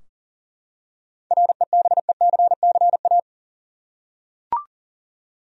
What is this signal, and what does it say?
Morse code, heard as a keyed tone, about 40 wpm, 700 Hz. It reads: REBECCA
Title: USA Common Names - 40wpm